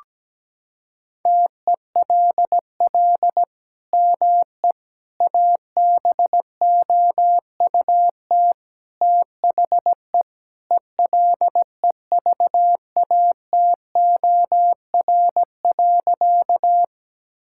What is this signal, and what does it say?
Morse code, heard as a keyed tone, about 17 wpm, 700 Hz. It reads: TELL ME ABOUT THE ELEVATOR.
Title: Sentences from Top 1600 Words 17wpm